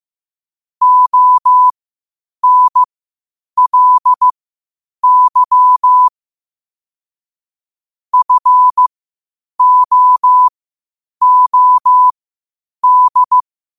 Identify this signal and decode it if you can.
Morse code, heard as a keyed tone, about 15 wpm, spaced as for 8 wpm, 1000 Hz. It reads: ONLY FOOD
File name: qotdfiles / 8wpm